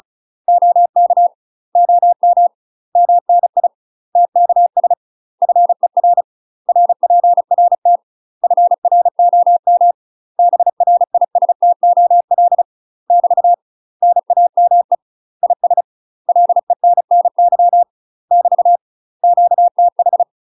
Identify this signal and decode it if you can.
Morse code, heard as a keyed tone, about 36 wpm, 700 Hz. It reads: OK OM MNI TKS FER RPRT FROM BRISTOL <BT> NAME IS LENNY <BT> QTH
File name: UK MASTER QSOs TRAINING / 35WPM / UK MASTER QSOs 0013